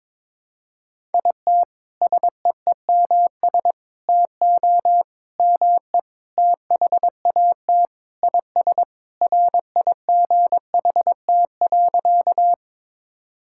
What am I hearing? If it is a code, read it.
Morse code, heard as a keyed tone, about 22 wpm, 700 Hz. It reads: IT SEEMS TO ME THAT IS RIGHT.
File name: practice / 22wpm / Sentences with 4-Word Phrases - 22wpm